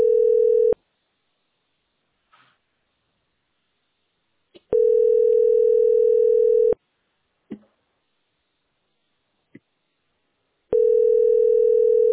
Ringback tone (North America), 440 + 480 Hz.